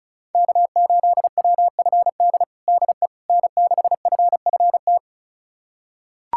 Morse code, heard as a keyed tone, about 34 wpm, 700 Hz. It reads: K8WFD DE N6FFT
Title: USA Callsign DE Callsign - 5 Characters - 35wpm